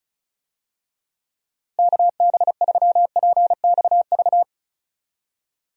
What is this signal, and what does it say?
Morse code, heard as a keyed tone, about 36 wpm, 700 Hz. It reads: KB3PXV
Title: USA Call Signs - 6 Characters - Repeated 3x - 35wpm